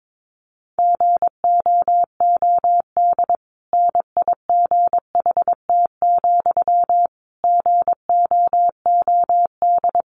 Morse code, heard as a keyed tone, about 22 wpm, 700 Hz. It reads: GOOD NIGHT, GOOD
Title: Sentences from Top 1500 Words 22wpm